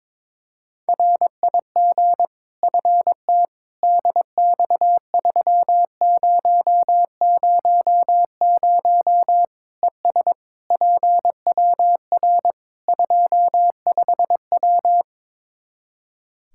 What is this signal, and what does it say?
Morse code, heard as a keyed tone, about 22 wpm, 700 Hz. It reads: RIG FT DX3000 ES PWR 25W
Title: Practice QSO - Exchange 2 - Info - 22wpm